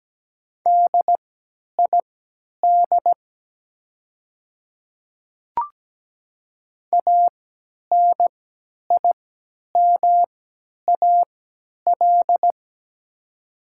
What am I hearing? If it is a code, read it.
Morse code, heard as a keyed tone, about 17 wpm, spaced as for 10 wpm, 700 Hz. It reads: DID ANIMAL
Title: 3x Character Spacing - Top 300 Words - 17wpm